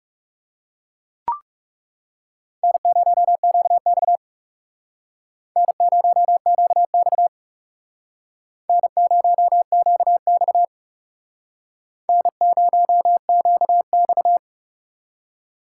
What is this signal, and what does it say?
Morse code, heard as a keyed tone, about 38 wpm, 700 Hz. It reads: N0QX N0QX N0QX N0QX